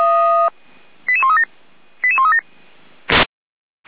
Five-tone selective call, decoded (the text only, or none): 88137 88137